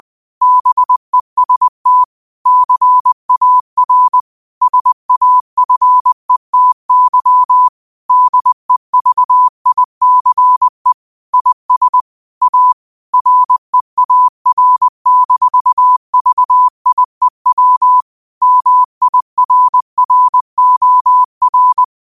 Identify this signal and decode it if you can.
Morse code, heard as a keyed tone, about 20 wpm, 1000 Hz. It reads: BEST CAR SAFETY DEVICE IS A REAR-VIEW MIRROR
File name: qotdfiles / 20wpm